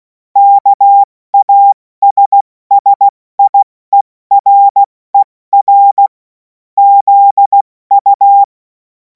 Morse code, heard as a keyed tone, about 16 wpm, 800 Hz. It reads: KASSIERER ZU
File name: Morsekurs / Witze / Witz 02. Juni 2026 70 F80